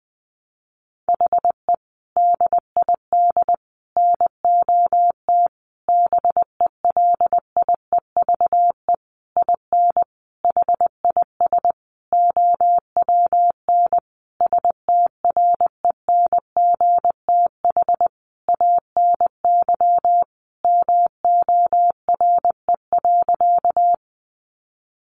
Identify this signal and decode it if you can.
Morse code, heard as a keyed tone, about 20 wpm, 700 Hz. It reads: HE DID NOT BELIEVE IN HIS OWN STRENGTH ANY MORE.
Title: Sentences from Top 1300 Words 20wpm